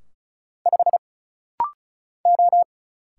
Morse code, heard as a keyed tone, about 34 wpm, 700 Hz. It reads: 5 O
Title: Single Letter-Number Alt - Rapid-Fire 35wpm